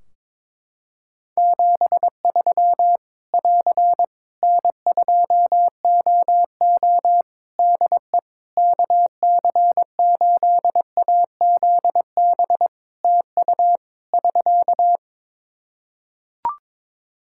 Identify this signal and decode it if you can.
Morse code, heard as a keyed tone, about 21 wpm, 700 Hz. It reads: 73 <AR> N2OO DE KC8AZB TU <SK>
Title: Practice QSO - Exchange 4 - Ending - 22wpm